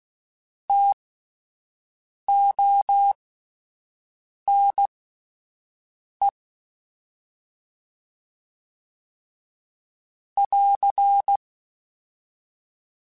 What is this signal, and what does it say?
Morse code, heard as a keyed tone, about 16 wpm, spaced as for 5 wpm, 800 Hz. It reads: TONE <AR>
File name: T Prefixes / T mp3s